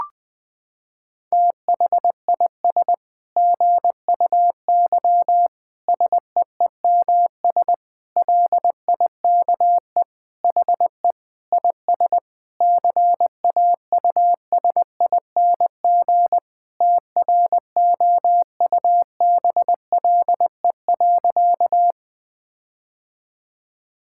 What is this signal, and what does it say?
Morse code, heard as a keyed tone, about 20 wpm, 700 Hz. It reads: THIS GUY SEEMS LIKE HE IS CAUSING TROUBLE.
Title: Sentences with 9-Word Phrases - 20wpm